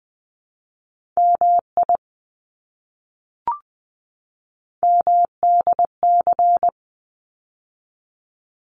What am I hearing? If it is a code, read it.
Morse code, heard as a keyed tone, about 20 wpm, 700 Hz. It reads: MI MDC
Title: ARRL Sections 20wpm